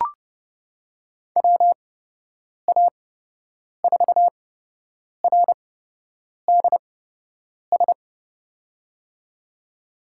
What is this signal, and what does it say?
Morse code, heard as a keyed tone, about 31 wpm, spaced as for 8 wpm, 700 Hz. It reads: WA4RDS
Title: USA Call Signs - 6 Characters - 8x Character Spacing - 30wpm